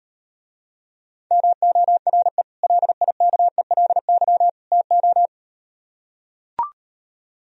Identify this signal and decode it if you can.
Morse code, heard as a keyed tone, about 38 wpm, 700 Hz. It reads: MORE LIKELY TO